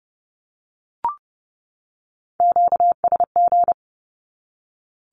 Morse code, heard as a keyed tone, about 30 wpm, 700 Hz. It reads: QSG
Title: Q-Codes 30wpm